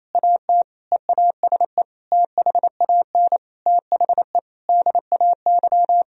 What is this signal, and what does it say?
Morse code, heard as a keyed tone, about 27 wpm, 700 Hz. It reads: AT EASE THAN THE DAY
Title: Sentences from Top 1000 Words - Review Part 1 - 28wpm